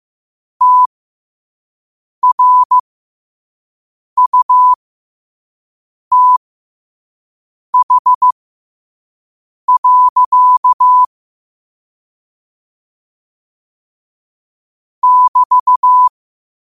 Morse code, heard as a keyed tone, about 15 wpm, spaced as for 5 wpm, 1000 Hz. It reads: TRUTH. <BT>
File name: qotdfiles / 5wpm